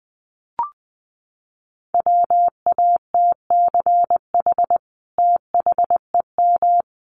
Morse code, heard as a keyed tone, about 20 wpm, 700 Hz. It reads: WATCH THEM
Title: Sentences from Top 1400 Words 20wpm